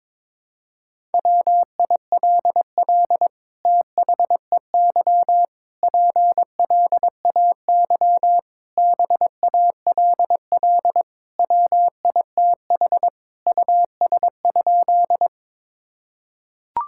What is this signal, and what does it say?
Morse code, heard as a keyed tone, about 22 wpm, 700 Hz. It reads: WILL THEY PLAY BALL WITH US?
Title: Sentences from Top 1000 Words - Review Part 6 - 22wpm